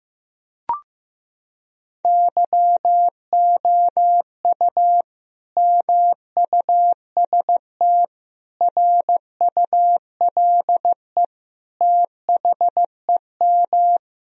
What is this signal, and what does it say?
Morse code, heard as a keyed tone, about 16 wpm, 700 Hz. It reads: YOU MUST RULE THEM
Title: Sentences from Top 1000 Words - Review Part 2 - 15wpm